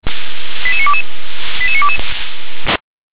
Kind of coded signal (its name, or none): five-tone selective call